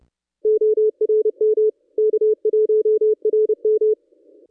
Morse code, heard as a keyed tone, about 29 wpm, 430 Hz. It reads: ORM K1RM
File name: contests / ss-cw-k5tr-op-n5rz -1200